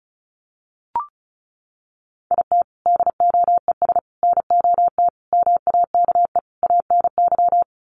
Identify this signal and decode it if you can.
Morse code, heard as a keyed tone, about 34 wpm, 700 Hz. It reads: IT DOES NOT MAKE ANY
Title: Sets of 5 Words Encore2 35wpm